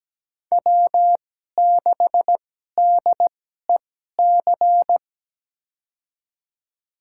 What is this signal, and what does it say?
Morse code, heard as a keyed tone, about 17 wpm, spaced as for 12 wpm, 700 Hz. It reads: W6DEC
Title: USA Call Signs - 5 Characters - 2x Character Spacing - 17wpm